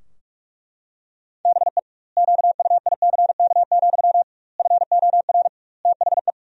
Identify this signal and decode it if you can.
Morse code, heard as a keyed tone, about 42 wpm, 700 Hz. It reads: BE QUICK, FOR THE
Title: 2x Word-Spacing - Sentences from Top 500 Words - 45wpm